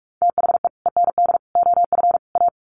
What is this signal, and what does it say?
Morse code, heard as a keyed tone, about 44 wpm, 700 Hz. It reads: THE END OF A
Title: Sets of 4 Words 45wpm